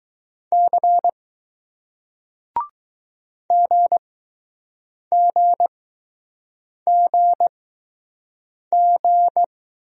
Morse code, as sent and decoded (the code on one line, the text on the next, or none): -.-. / --. --. --. --.
C GGGG